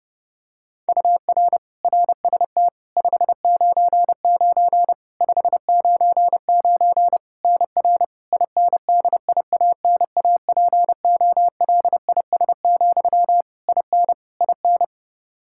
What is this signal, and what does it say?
Morse code, heard as a keyed tone, about 32 wpm, 700 Hz. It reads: UR RST 599 599 NR INDIANAPOLIS, IN IN
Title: Practice QSO - Exchange 1 - Beginning - 30wpm